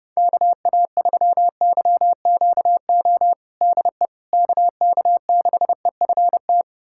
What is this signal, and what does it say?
Morse code, heard as a keyed tone, about 32 wpm, 700 Hz. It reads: KA3YQO DE KK6EFT
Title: USA Callsign DE Callsign - 6 Characters - 30wpm